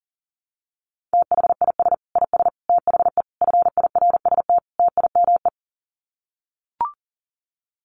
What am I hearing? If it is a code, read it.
Morse code, heard as a keyed tone, about 38 wpm, 700 Hz. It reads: THIS IS THE FIRST TIME
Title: Sets of 5 Words 40wpm